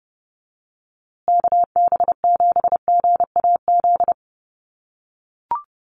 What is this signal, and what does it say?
Morse code, heard as a keyed tone, about 30 wpm, 700 Hz. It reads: KB7GAZ
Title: USA Call Signs Repeated 2x 30wpm